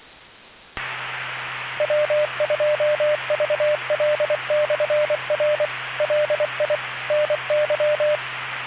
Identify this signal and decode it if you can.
Morse code, heard as a keyed tone, about 24 wpm, 590 Hz, with radio noise. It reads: W2VL/R LI NY